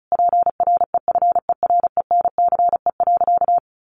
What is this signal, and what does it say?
Morse code, heard as a keyed tone, about 34 wpm, 700 Hz. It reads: PREFERENCE.
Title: Sentences from Top 1500 Words 35wpm